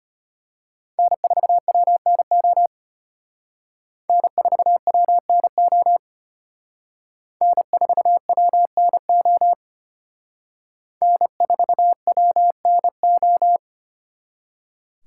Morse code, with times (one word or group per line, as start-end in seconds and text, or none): 1.0-2.7 N4WNO
4.1-6.0 N4WNO
7.4-9.5 N4WNO
11.0-13.6 N4WNO